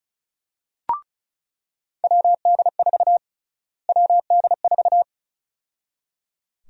Morse code, heard as a keyed tone, about 36 wpm, 700 Hz. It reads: WD4 WD4
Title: USA Call Sign Prefixes - Repeated 2x - 35wpm